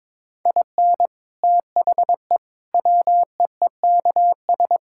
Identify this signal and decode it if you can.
Morse code, heard as a keyed tone, about 22 wpm, 700 Hz. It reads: IN THE WEEKS
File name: Sentences with 7-Word Phrases - 22wpm